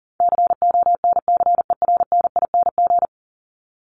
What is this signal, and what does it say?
Morse code, heard as a keyed tone, about 40 wpm, 700 Hz. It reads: CONCERNING
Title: ING Suffix - Top 100 Words with Base Words 40wpm